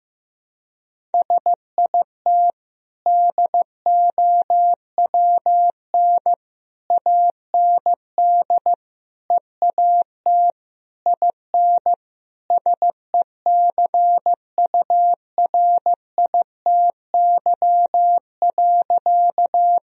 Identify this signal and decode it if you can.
Morse code, heard as a keyed tone, about 16 wpm, 700 Hz. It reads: SIT DOWN AND EAT IN SECURITY.
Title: Sentences from Top 1000 Words - Review Part 4 - 15wpm